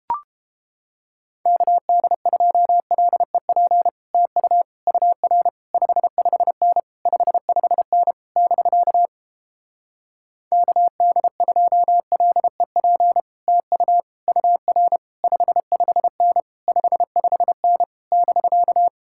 Morse code, heard as a keyed tone, about 29 wpm, 700 Hz. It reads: KD2LEP TU UR 55N 55N <BK> KD2LEP TU UR 55N 55N <BK>